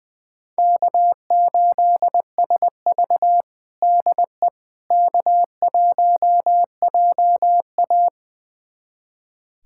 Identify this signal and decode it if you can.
Morse code, heard as a keyed tone, about 21 wpm, 700 Hz. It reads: K8SV DE K1JA
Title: USA Callsign DE Callsign - 4 Characters - 20wpm